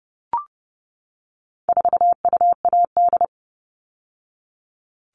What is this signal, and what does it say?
Morse code, heard as a keyed tone, about 30 wpm, 700 Hz. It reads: 4UAD
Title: All Cut Numbers - 4-Digit 30wpm